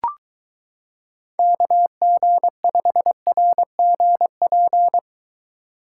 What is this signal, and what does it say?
Morse code, heard as a keyed tone, about 23 wpm, 700 Hz. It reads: KG5RGP